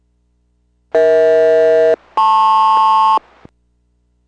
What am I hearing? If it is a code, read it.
Aircraft SELCAL, heard as two pairs of tones. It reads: EH-LP